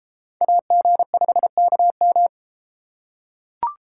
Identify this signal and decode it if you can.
Morse code, heard as a keyed tone, about 33 wpm, 700 Hz. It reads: AG5KM